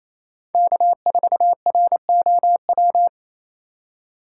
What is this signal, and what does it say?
Morse code, heard as a keyed tone, about 34 wpm, 700 Hz. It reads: K4ROW
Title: Speed Racing - USA 1x3 Call Signs 28wpm